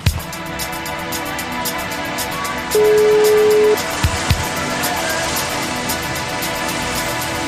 Ringback tone (Europe, 425 Hz), with noise.